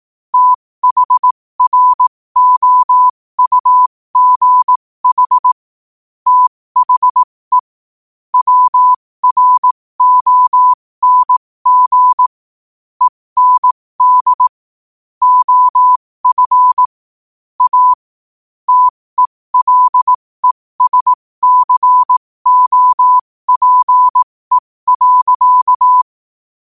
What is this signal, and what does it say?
Morse code, heard as a keyed tone, about 18 wpm, spaced as for 15 wpm, 1000 Hz. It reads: THROUGH THE WRONG END OF A TELESCOPE.
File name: 15wpm